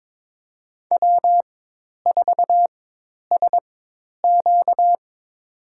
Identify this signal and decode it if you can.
Morse code, heard as a keyed tone, about 22 wpm, spaced as for 10 wpm, 700 Hz. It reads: W4SQ
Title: USA Call Signs - 4x Character Spacing - 22wpm